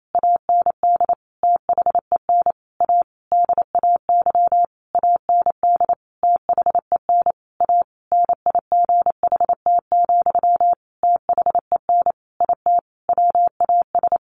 Morse code, heard as a keyed tone, about 28 wpm, 700 Hz. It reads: AND THEN A DAY AND THEN A NIGHT, THEN IT WAS
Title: A Sound Of Thunder 28wpm